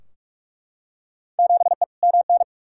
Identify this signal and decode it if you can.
Morse code, heard as a keyed tone, about 42 wpm, 700 Hz. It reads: 7E MN
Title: Field Day Exchanges Encore - 45wpm